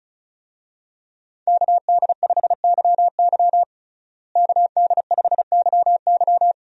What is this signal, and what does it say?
Morse code, heard as a keyed tone, about 34 wpm, 700 Hz. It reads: KD5YY KD5YY
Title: USA Call Signs - 5 Characters - Repeated 2x - 35wpm